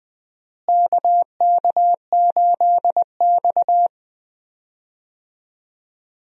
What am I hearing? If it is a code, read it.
Morse code, heard as a keyed tone, about 20 wpm, 700 Hz. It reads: KK8X
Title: USA Call Signs - 4 Characters - Repeated 2x - 20wpm